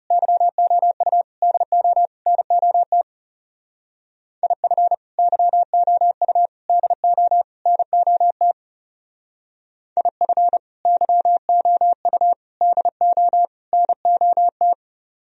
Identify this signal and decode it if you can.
Morse code, heard as a keyed tone, about 38 wpm, 700 Hz. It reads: YOU DO NOT IF YOU DO NOT IF YOU DO NOT